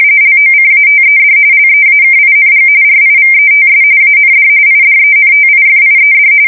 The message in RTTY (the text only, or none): LOCATED LEAKING RIVETS AND IDENTIFIED